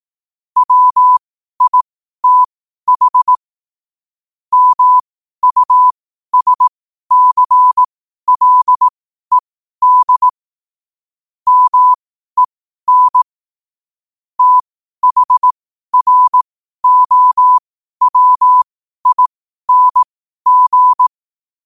Morse code, heard as a keyed tone, about 18 wpm, spaced as for 12 wpm, 1000 Hz. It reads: WITH MUSCLED MEN THROWING
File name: qotdfiles / 12wpm